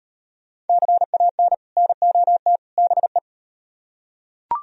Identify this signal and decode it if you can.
Morse code, heard as a keyed tone, about 38 wpm, 700 Hz. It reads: CAN NOT BE